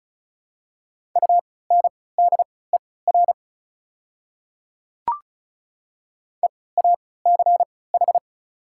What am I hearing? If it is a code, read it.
Morse code, heard as a keyed tone, about 36 wpm, spaced as for 20 wpm, 700 Hz. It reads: UNDER EACH
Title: 3x Character Spacing - Top 200 Words - 35wpm